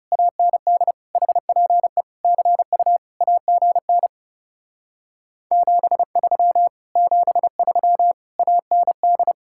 Morse code, heard as a keyed tone, about 40 wpm, 700 Hz. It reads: AND HPE CU AGN 73 73 AND